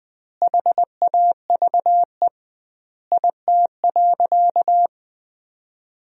Morse code, heard as a keyed tone, about 21 wpm, 700 Hz. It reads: HAVE IT.
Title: 2x Word-Spacing - Sentences from Top 100 Words - 20wpm